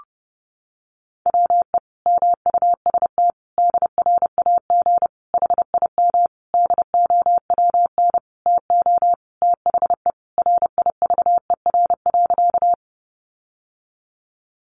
Morse code, heard as a keyed tone, about 28 wpm, 700 Hz. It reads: WE MUST DRAG HIM DOWN TO THE RIVER.
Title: Sentences from Top 1500 Words 30wpm